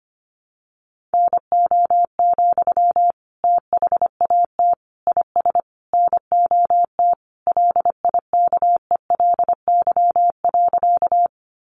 Morse code, heard as a keyed tone, about 25 wpm, 700 Hz. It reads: NO, THAT IS NOT LIKELY.
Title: Sentences from Top 600 Words 25wpm